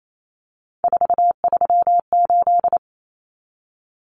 Morse code, heard as a keyed tone, about 28 wpm, 700 Hz. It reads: 438
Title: Common Cut Numbers - 3-Digit 28wpm